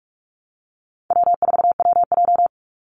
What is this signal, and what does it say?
Morse code, heard as a keyed tone, about 44 wpm, 700 Hz. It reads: W4WJ
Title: Call Signs Repeated 2x - 45wpm